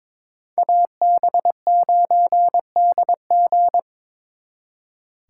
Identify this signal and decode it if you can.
Morse code, heard as a keyed tone, about 26 wpm, 700 Hz. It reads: AB9DG
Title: Speed Racing - USA 2x2 Call Signs 22wpm